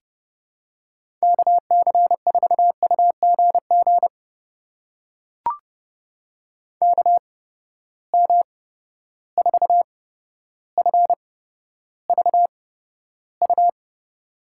Morse code, heard as a keyed tone, about 32 wpm, spaced as for 13 wpm, 700 Hz. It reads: KC4UGG KM4FVU
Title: USA Call Signs - 6 Characters - 8x Character Spacing - 30wpm